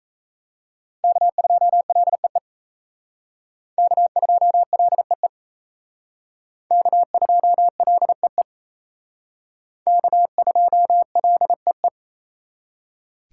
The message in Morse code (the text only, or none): K2LEE K2LEE K2LEE K2LEE